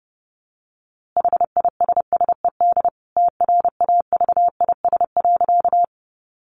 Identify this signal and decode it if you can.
Morse code, heard as a keyed tone, about 30 wpm, 700 Hz. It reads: HISSED TRAVIS.
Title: A Sound Of Thunder 30wpm